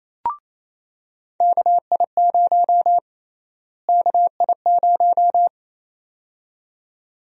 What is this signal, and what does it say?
Morse code, heard as a keyed tone, about 27 wpm, 700 Hz. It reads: KI0 KI0
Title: USA Call Sign Prefixes - Repeated 2x - 28wpm